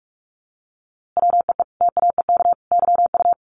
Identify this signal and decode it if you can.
Morse code, heard as a keyed tone, about 44 wpm, 700 Hz. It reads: WEE TAEK YU
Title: A Sound Of Thunder 45wpm